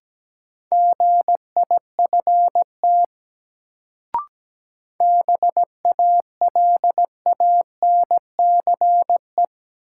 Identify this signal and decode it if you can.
Morse code, heard as a keyed tone, about 17 wpm, 700 Hz. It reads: GIFT BALANCE
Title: Sentences from Top 900 Words - New Word Drill - 17wpm